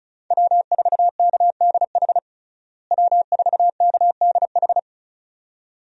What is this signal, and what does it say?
Morse code, heard as a keyed tone, about 36 wpm, 700 Hz. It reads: W4KDH W4KDH
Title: USA Call Signs - 5 Characters - Repeated 2x - 35wpm